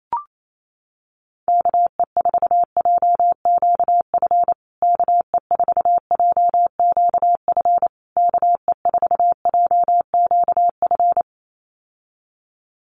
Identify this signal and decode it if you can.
Morse code, heard as a keyed tone, about 28 wpm, 700 Hz. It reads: KE4JQF KE4JQF KE4JQF
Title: USA Call Signs Repeated 3x 28wpm